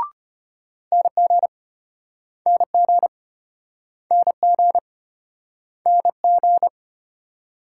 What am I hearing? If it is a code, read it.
Morse code, heard as a keyed tone, about 34 wpm, 700 Hz. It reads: NG NG NG NG